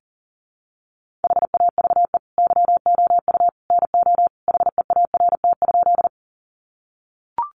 Morse code, heard as a keyed tone, about 40 wpm, 700 Hz. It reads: HAVE YOU NO HEART?
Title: Sentences from Top 1200 Words 40wpm